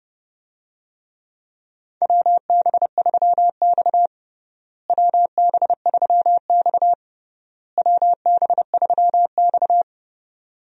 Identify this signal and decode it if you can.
Morse code, heard as a keyed tone, about 30 wpm, 700 Hz. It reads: WB3X WB3X WB3X
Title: USA Call Signs - 4 Characters - Repeated 3x - 30wpm